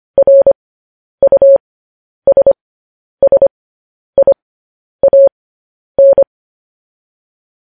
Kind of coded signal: Morse code